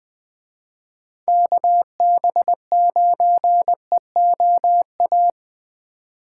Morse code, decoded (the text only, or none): KB9EOA